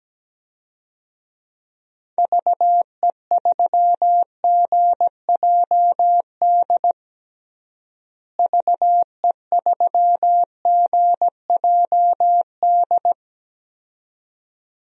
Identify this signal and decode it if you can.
Morse code, heard as a keyed tone, about 17 wpm, 700 Hz. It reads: VE3GJD VE3GJD
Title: DXCC-100 Call Signs - 6 Characters - Repeated 2x - 17wpm